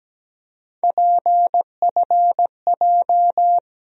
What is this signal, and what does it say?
Morse code, heard as a keyed tone, about 17 wpm, 700 Hz. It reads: PFJ
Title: USA Call Sign Postfixes - 3 Letters 17wpm